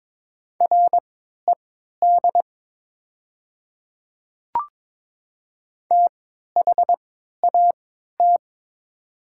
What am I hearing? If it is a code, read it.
Morse code, heard as a keyed tone, about 22 wpm, spaced as for 13 wpm, 690 Hz. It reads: RED THAT
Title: 3x Character Spacing - Top 500 Words - 22wpm